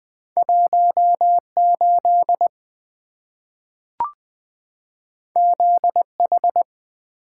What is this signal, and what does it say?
Morse code, heard as a keyed tone, about 20 wpm, 700 Hz. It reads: 18 ZH
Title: Two Letter-Number Combinations - 20wpm